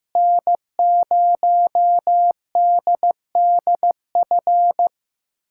Morse code, heard as a keyed tone, about 15 wpm, 700 Hz. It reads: N0DDF
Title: USA Callsign DE Callsign 15wpm